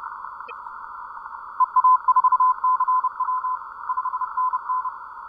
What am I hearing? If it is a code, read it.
Morse code, heard as a keyed tone, about 29 wpm, 1020 Hz. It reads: EA4CW 5NN